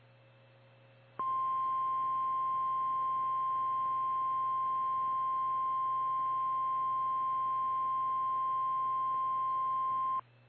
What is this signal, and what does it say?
Emergency Alert System (SAME) tones (the 1050 Hz attention tone). It is heard from 1.1 to 10.2 s.
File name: eas / KKPRKXPN / KKPRKXPN